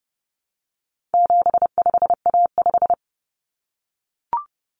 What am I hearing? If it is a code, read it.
Morse code, heard as a keyed tone, about 30 wpm, 700 Hz. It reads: 75A5
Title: Common Cut Numbers - 4-Digit 30wpm